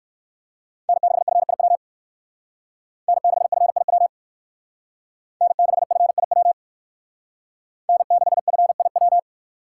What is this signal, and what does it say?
Morse code, heard as a keyed tone, about 57 wpm, 700 Hz. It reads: N6FIW N6FIW N6FIW N6FIW